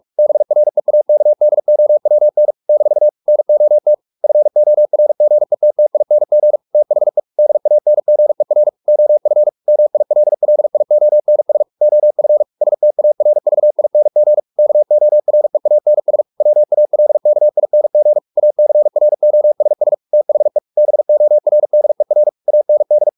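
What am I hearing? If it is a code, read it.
Morse code, heard as a keyed tone, about 44 wpm, 600 Hz. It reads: BREAKDOWN - NOT FORGETTING THE DANGER OF MILLIONS OF STARVING KOREANS WALKING ACROSS THE BORDER AND
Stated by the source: Morse practice at 45 wpm, the novel Never by Ken Follett in English, from CW med Gnister.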